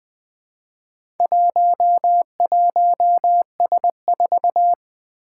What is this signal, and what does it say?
Morse code, heard as a keyed tone, about 20 wpm, 700 Hz. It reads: 11S4
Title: Four Challenging Letter-Number Combinations 20wpm